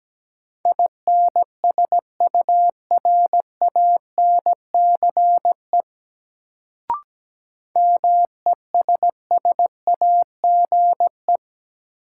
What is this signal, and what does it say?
Morse code, heard as a keyed tone, about 17 wpm, 700 Hz. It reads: INSURANCE MESSAGE
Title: Sentences from Top 1100 Words - New Word Drill Alt - 17wpm